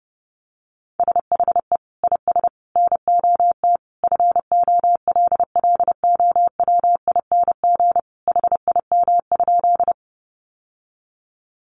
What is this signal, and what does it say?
Morse code, heard as a keyed tone, about 30 wpm, 700 Hz. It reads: SHE IS NOT FOLLOWING HIM?
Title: Sentences from Top 700 Words 30wpm